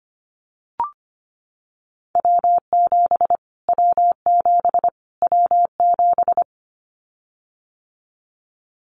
Morse code, heard as a keyed tone, about 25 wpm, 700 Hz. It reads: W7 W7 W7
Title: Callsign Prefixes for Top 25 DXCC - Repeated 3x - 25wpm